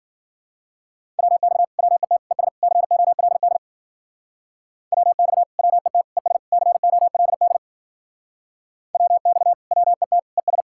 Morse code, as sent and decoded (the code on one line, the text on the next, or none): .-- -..- / .-- . - / . ... / -.-. --- .-.. -.. / .-- -..- / .-- . - / . ... / -.-. --- .-.. -.. / .-- -..- / .-- . - / . ...
WX WET ES COLD WX WET ES COLD WX WET ES